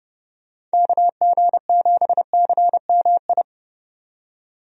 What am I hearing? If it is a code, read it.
Morse code, heard as a keyed tone, about 30 wpm, 700 Hz. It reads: KG7CMI
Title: USA Call Signs - 6 Characters - 30wpm